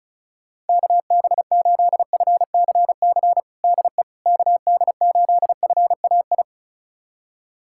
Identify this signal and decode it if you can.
Morse code, heard as a keyed tone, about 36 wpm, 700 Hz. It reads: KB8FCC DE KD8FAI
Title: USA Callsign DE Callsign 35wpm